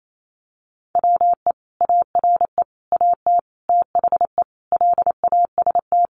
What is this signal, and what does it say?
Morse code, heard as a keyed tone, about 28 wpm, 700 Hz. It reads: WE ARE AT THE LAST
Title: Sentences from Top 1000 Words 28wpm